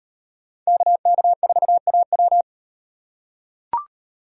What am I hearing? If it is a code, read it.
Morse code, heard as a keyed tone, about 36 wpm, 700 Hz. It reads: KK4AW